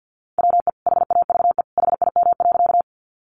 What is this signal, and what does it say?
Morse code, heard as a keyed tone, about 50 wpm, 700 Hz. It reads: WE HAVE HIM.
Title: Sentences from Top 100 Words 50wpm